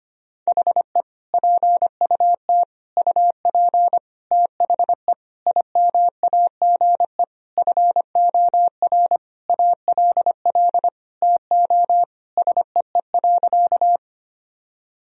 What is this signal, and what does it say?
Morse code, heard as a keyed tone, about 25 wpm, 700 Hz. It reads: HE PUT UP THE IMAGE FOR ALL TO SEE.
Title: Sentences from Top 1000 Words - Review Part 4 - 25wpm